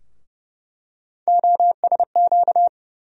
Morse code, heard as a keyed tone, about 30 wpm, 700 Hz. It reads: OSQ
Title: USA Call Sign Postfixes - 3 Letters 30wpm